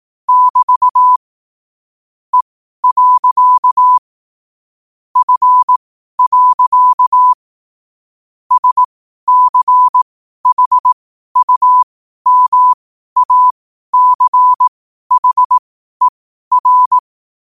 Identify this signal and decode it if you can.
Morse code, heard as a keyed tone, about 18 wpm, spaced as for 12 wpm, 1000 Hz. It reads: <BT> E. F. SCHUMACHER
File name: qotdfiles / 12wpm